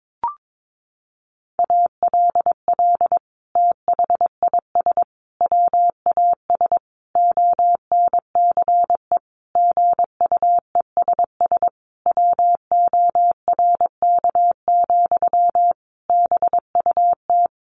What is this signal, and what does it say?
Morse code, heard as a keyed tone, about 22 wpm, 700 Hz. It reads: ALL THIS WAS ONCE GUESS WORK, BUT
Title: Sentences from Top 1500 Words 22wpm